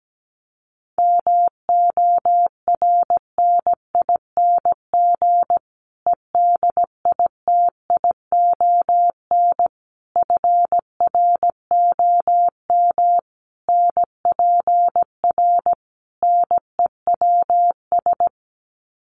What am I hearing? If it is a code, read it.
Morse code, heard as a keyed tone, about 17 wpm, 700 Hz. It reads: MORNING EDITION FROM NPR NEWS
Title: Sets of 5 Words Encore 17wpm